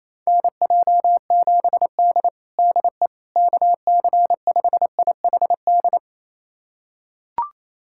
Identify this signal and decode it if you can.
Morse code, heard as a keyed tone, about 28 wpm, 700 Hz. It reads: NJ7D DE KC5IHD
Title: USA Callsign DE Callsign 28wpm